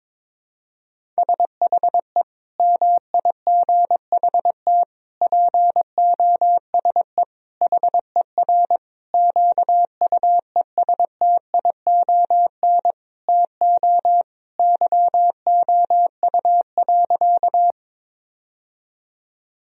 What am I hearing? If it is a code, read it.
Morse code, heard as a keyed tone, about 22 wpm, 700 Hz. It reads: SHE MIGHT POSE HER QUESTION TO YOU.
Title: Sentences from Top 1000 Words - Review Part 2 - 22wpm